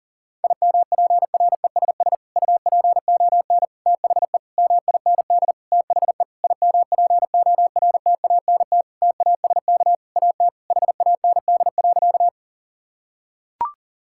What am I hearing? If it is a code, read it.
Morse code, heard as a keyed tone, about 40 wpm, 700 Hz. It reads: IMPRESS UPON THE MIND THE IMPORTANT TASK AT HAND.
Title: Sentences from Top 1600 Words 40wpm